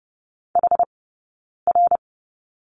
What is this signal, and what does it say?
Morse code, heard as a keyed tone, about 30 wpm, spaced as for 9 wpm, 690 Hz. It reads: HR
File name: course / 30:10wpm / 086 Lesson 15d - Number 5 - Two Characters 30wpm